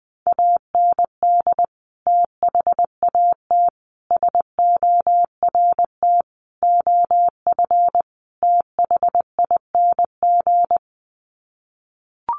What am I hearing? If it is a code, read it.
Morse code, heard as a keyed tone, about 20 wpm, 700 Hz. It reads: AND THAT SORT OF THING
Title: Sets of 5 Words Encore2 20wpm